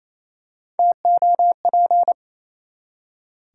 Morse code, heard as a keyed tone, about 28 wpm, 700 Hz. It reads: TOP
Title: Three Letter Words - 28wpm